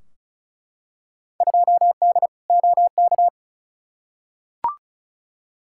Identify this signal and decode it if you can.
Morse code, heard as a keyed tone, about 36 wpm, 700 Hz. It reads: 2D OK
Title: Field Day Exchanges Encore - 35wpm